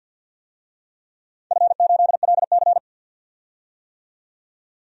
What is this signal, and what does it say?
Morse code, heard as a keyed tone, about 50 wpm, 700 Hz. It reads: F8LC
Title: DXCC-100 Call Signs - 4 Characters - 50wpm